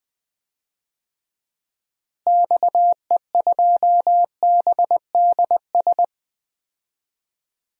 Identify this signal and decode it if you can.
Morse code, heard as a keyed tone, about 19 wpm, 700 Hz. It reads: XE2BDS
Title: DXCC-100 Call Signs - 6 Characters - Repeated 3x - 20wpm